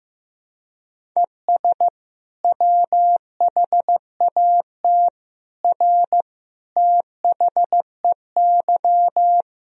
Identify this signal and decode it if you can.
Morse code, heard as a keyed tone, about 16 wpm, 700 Hz. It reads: ES WHAT R THEY
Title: Sentences from Top 200 Words and Abbreviations 15wpm